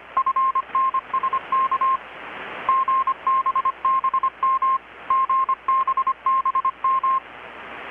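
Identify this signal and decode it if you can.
Morse code, heard as a keyed tone, about 25 wpm, 1060 Hz. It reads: FNSK GBBM GBBM
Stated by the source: Morse from KPH's Night of Nights XX maritime radiotelegraphy event, received in Chiapas, Mexico about 3,800 km away on a Kenwood R-600 with a grounded 10 m long-wire antenna.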